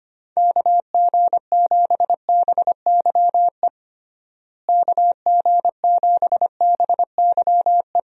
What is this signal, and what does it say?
Morse code, heard as a keyed tone, about 25 wpm, 700 Hz. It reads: KG7BYE KG7BYE
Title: USA Call Signs - 6 Characters - Repeated 2x - 25wpm